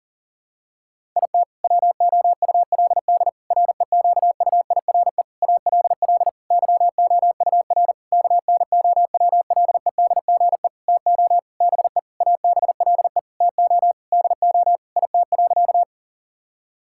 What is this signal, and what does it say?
Morse code, heard as a keyed tone, about 40 wpm, 700 Hz. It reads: IT WOULD REQUIRE ALL YOUR KNOWLEDGE TO BE ABLE TO DO IT.
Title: Sentences from Top 1000 Words - Review Part 6 - 40wpm